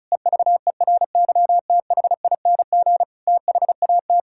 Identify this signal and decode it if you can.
Morse code, heard as a keyed tone, about 36 wpm, 700 Hz. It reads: EVERYTHING THAT
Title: Sentences from Top 1000 Words - Review Part 5 - 35wpm